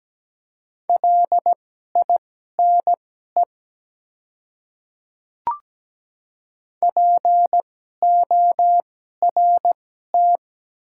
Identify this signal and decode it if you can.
Morse code, heard as a keyed tone, about 17 wpm, spaced as for 12 wpm, 700 Hz. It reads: LINE PORT
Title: 2x Character Spacing - Top 200 Words - 17wpm